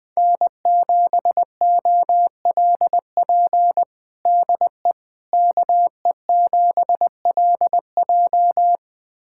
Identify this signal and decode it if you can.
Morse code, heard as a keyed tone, about 21 wpm, 700 Hz. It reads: N7OLP DE KE7LJ
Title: USA Callsign DE Callsign - 5 Characters - 20wpm